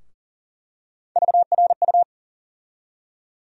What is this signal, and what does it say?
Morse code, heard as a keyed tone, about 40 wpm, 700 Hz. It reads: VRU
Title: USA Call Sign Postfixes - 3 Letters 40wpm